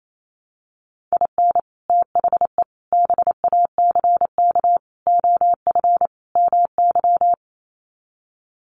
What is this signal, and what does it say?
Morse code, heard as a keyed tone, about 28 wpm, 700 Hz. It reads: IN THE BACK OF MY
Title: Sets of 5 Words Encore 28wpm